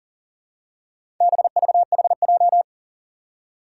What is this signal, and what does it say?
Morse code, heard as a keyed tone, about 40 wpm, 700 Hz. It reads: BVHJ
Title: Four Challenging Letter-Number Combinations 40wpm